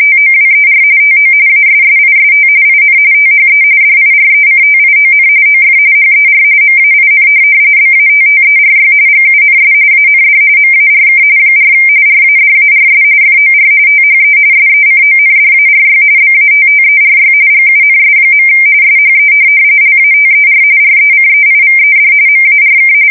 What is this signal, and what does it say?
RTTY, about 45.45 baud. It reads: TOWER TO ASSIST IN THE DIRECTION OF HER MAIN BATTERY GUNS AND MARK 4 RADARS FOR THE SECONDARY BATTERY GUNS. SHE LATER RECEIVED MARK